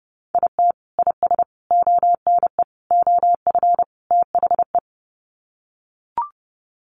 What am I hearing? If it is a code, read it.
Morse code, heard as a keyed tone, about 30 wpm, 700 Hz. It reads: IT IS ONE OF THE
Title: Sets of 5 Words 30wpm